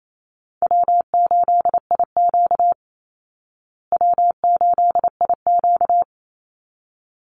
Morse code, heard as a keyed tone, about 28 wpm, 700 Hz. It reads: W8IQ W8IQ
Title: Call Signs Repeated 3x - 28wpm